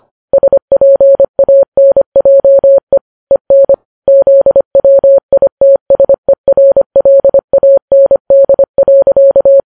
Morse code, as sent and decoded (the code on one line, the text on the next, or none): ... .--. .- -. .--- . / . -. / --.. .-- .. - ... .-. .-.. .- -. -.. .-.-.-
SPANJE EN ZWITSERLAND.